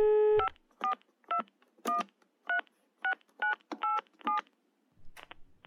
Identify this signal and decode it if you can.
DTMF, heard as telephone touch tones, tones of about 110 ms. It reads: 542136987